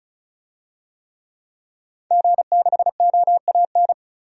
Morse code, heard as a keyed tone, about 36 wpm, 700 Hz. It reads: G6OAN
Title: DXCC-100 Call Signs - 5 Characters - 35wpm